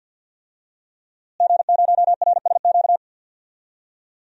Morse code, heard as a keyed tone, about 52 wpm, 700 Hz. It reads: G0RSQ
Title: DXCC-100 Call Signs - 5 Characters - Repeated 3x - 50wpm